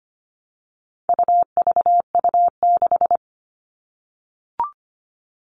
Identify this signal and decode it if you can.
Morse code, heard as a keyed tone, about 25 wpm, 700 Hz. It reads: UVU6
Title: All Cut Numbers - 4-Digit 25wpm